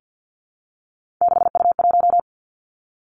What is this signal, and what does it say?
Morse code, heard as a keyed tone, about 50 wpm, 700 Hz. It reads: 6U1
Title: Three Challenging Letter-Number Combinations 50wpm